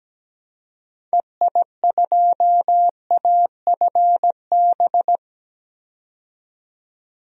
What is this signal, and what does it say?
Morse code, heard as a keyed tone, about 17 wpm, 700 Hz. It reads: EI2AFB
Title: DXCC-100 Call Signs - 6 Characters - 17wpm